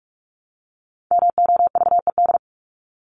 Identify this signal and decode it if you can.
Morse code, heard as a keyed tone, about 44 wpm, 700 Hz. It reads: MOVED
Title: ED Suffix - Top 100 Words with Base Words 45wpm